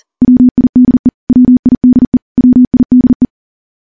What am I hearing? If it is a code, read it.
Morse code, heard as a keyed tone, about 42 wpm, 250 Hz. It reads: WIDE WIDE WIDE
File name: morse / morse-training-40WPM